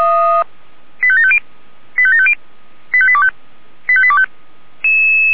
Five-tone selective call (ZVEI), heard as quick runs of five tones, tones of about 70 ms. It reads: 85750 85750 85825 85825